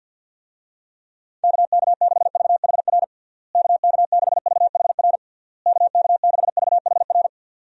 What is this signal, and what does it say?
Morse code, heard as a keyed tone, about 48 wpm, 700 Hz. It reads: KK6VHR KK6VHR KK6VHR
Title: USA Call Signs - 6 Characters - Repeated 3x - 50wpm